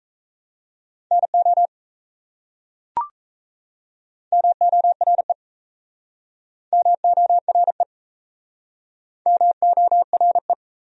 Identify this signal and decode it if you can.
Morse code, heard as a keyed tone, about 37 wpm, 700 Hz. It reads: NO MORE MORE MORE